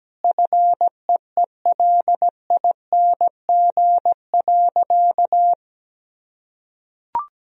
Morse code, heard as a keyed tone, about 17 wpm, 700 Hz. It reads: FEELING.
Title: Sentences from Top 1000 Words - Review Part 3 - 17wpm